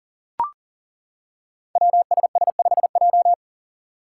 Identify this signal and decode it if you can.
Morse code, heard as a keyed tone, about 42 wpm, 700 Hz. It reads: WSS5J
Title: Five Challenging Letter-Number Combinations 40wpm